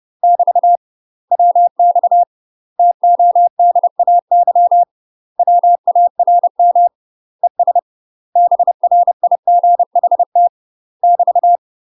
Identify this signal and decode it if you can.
Morse code, heard as a keyed tone, about 30 wpm, 700 Hz. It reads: <BT> WX TODAY WARM ES BRIGHT <BT>
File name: UK MASTER QSOs TRAINING / 30WPM / UK MASTER QSOs 0043